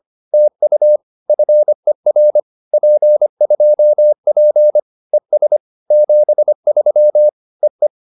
Morse code, heard as a keyed tone, about 25 wpm, 600 Hz. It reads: TU FER P2P ES 73 EE
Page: advanced-practice-files